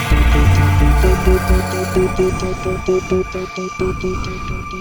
There is no code.